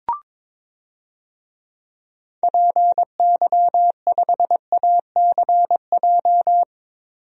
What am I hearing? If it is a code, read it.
Morse code, heard as a keyed tone, about 22 wpm, 700 Hz. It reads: PY5ACJ
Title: DXCC-100 Call Signs - 6 Characters - 22wpm